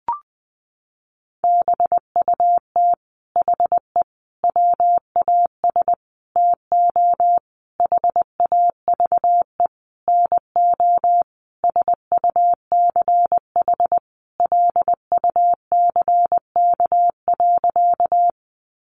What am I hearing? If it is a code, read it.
Morse code, heard as a keyed tone, about 20 wpm, 700 Hz. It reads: BUT HE WAS TO HAVE NO SUCH LUCK.
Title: Sentences from Top 1500 Words 20wpm